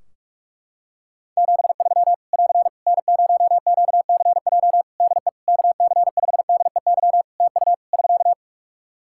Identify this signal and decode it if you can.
Morse code, heard as a keyed tone, about 41 wpm, 700 Hz. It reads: 73 <AR> N0QCJ DE KC5DEY TU <SK>
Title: Practice QSO - Exchange 4 - Ending - 45wpm